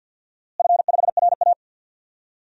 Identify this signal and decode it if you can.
Morse code, heard as a keyed tone, about 52 wpm, 700 Hz. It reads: F5RA
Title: DXCC-100 Call Signs - 4 Characters - Repeated 2x - 50wpm